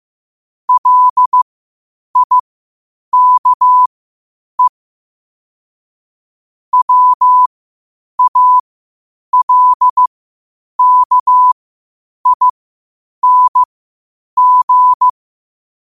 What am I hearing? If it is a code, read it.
Morse code, heard as a keyed tone, about 15 wpm, spaced as for 8 wpm, 1000 Hz. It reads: LIKE WALKING
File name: 8wpm